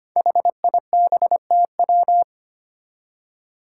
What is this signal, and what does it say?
Morse code, heard as a keyed tone, about 25 wpm, 700 Hz. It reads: HIBTW
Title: Five Challenging Letter-Number Combinations 25wpm